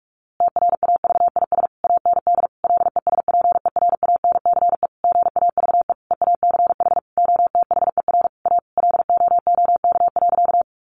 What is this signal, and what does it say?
Morse code, heard as a keyed tone, about 44 wpm, 700 Hz. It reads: TRAVIS AND LESPERANCE GAVE EACH OTHER A LOOK.
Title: A Sound Of Thunder 45wpm